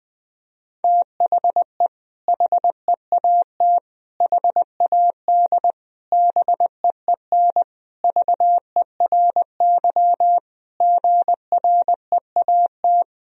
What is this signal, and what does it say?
Morse code, heard as a keyed tone, about 20 wpm, 700 Hz. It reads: THE HEAT HAD BEEN VERY GREAT
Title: Sentences from Top 1000 Words - Review Part 3 - 20wpm